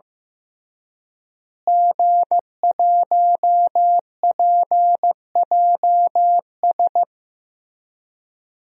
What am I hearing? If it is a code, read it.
Morse code, heard as a keyed tone, about 15 wpm, 700 Hz. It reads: G1PJS
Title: DXCC-100 Call Signs - 5 Characters - Repeated 3x - 15wpm